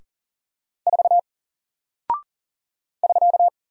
Morse code, heard as a keyed tone, about 39 wpm, 700 Hz. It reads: 4 <SK>
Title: Single Character (Letter:Number:Punctuation:Prosign) Alt 40wpm